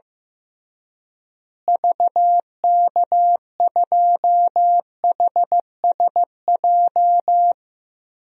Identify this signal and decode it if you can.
Morse code, heard as a keyed tone, about 15 wpm, 700 Hz. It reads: VK2HSJ
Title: DXCC-100 Call Signs - 6 Characters - Repeated 3x - 15wpm